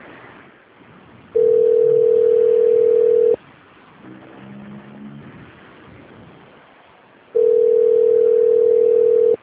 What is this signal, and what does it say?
Ringback tone (North America), 440 + 480 Hz.